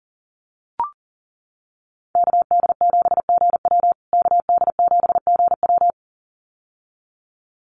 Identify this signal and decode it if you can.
Morse code, heard as a keyed tone, about 38 wpm, 700 Hz. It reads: KD7GW KD7GW
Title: USA Call Signs Repeated 2x 40wpm